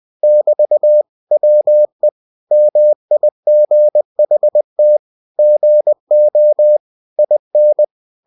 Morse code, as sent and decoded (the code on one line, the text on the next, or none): -...- / .-- . / -- .. --. .... - / --. --- / .. -.
<BT> WE MIGHT GO IN